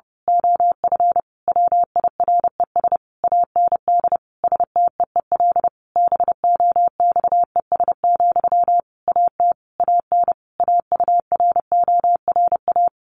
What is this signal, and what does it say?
Morse code, heard as a keyed tone, about 30 wpm, 700 Hz. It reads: OF WIRES AND STEEL BOXES, AT AN AURORA
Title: A Sound Of Thunder 30wpm